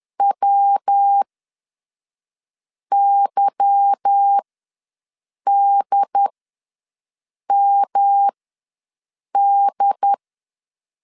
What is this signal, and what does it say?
Morse code, heard as a keyed tone, about 10 wpm, spaced as for 5 wpm, 800 Hz. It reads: WYDMD
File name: LEZIONI / 8 YJ45 / 8 YJ4 ws 10wpm 4020